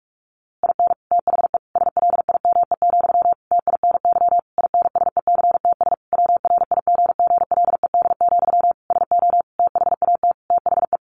Morse code, heard as a keyed tone, about 46 wpm, 700 Hz. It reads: IN THE SLIME, TINY INSECTS WRIGGLED, SO THAT THE
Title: A Sound Of Thunder 45wpm